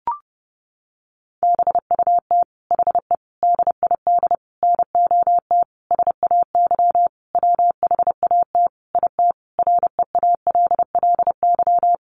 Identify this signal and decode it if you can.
Morse code, heard as a keyed tone, about 30 wpm, 700 Hz. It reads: BUT HE DID NOT SAY WHAT IT REALLY
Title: Sentences from Top 600 Words 30wpm